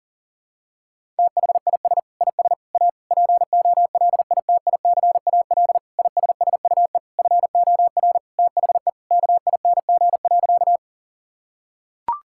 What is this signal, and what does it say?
Morse code, heard as a keyed tone, about 40 wpm, 700 Hz. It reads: THIS IS A POLITICAL ISSUE FOR THE KING.
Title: Sentences from Top 1000 Words - Review Part 6 - 40wpm